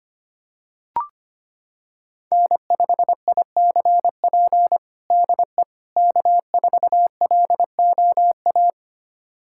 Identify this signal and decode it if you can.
Morse code, heard as a keyed tone, about 25 wpm, 700 Hz. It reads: N5ICP DE K4LOA
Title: USA Callsign DE Callsign - 5 Characters - 25wpm